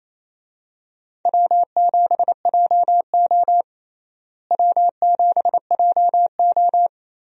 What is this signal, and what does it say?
Morse code, heard as a keyed tone, about 28 wpm, 700 Hz. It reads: W7JO W7JO
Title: USA Call Signs - 4 Characters - Repeated 2x - 28wpm